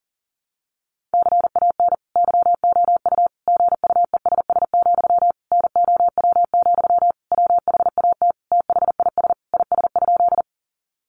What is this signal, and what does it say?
Morse code, heard as a keyed tone, about 40 wpm, 700 Hz. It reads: CAN YOU GUESS, NOW, WHAT THIS IS?
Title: Sentences from Top 1500 Words 40wpm